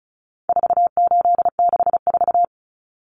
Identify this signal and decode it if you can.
Morse code, heard as a keyed tone, about 34 wpm, 700 Hz. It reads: 4864
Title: Numbers - 4-Digit 35wpm